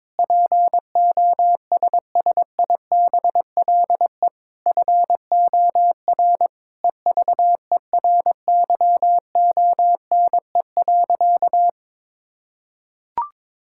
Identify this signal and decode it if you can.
Morse code, heard as a keyed tone, about 22 wpm, 700 Hz. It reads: POSSIBLE FOR EVERYONE.
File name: practice / 22wpm / Sentences with 8-Word Phrases - 22wpm